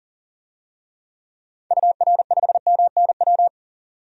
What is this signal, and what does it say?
Morse code, heard as a keyed tone, about 40 wpm, 700 Hz. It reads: UR5MNW
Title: DXCC-100 Call Signs - 6 Characters - 40wpm